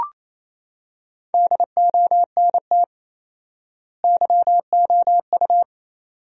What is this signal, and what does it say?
Morse code, heard as a keyed tone, about 28 wpm, 700 Hz. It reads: DONT YOU
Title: 4x Word-Spacing - Sentences from Top 300 Words - 28wpm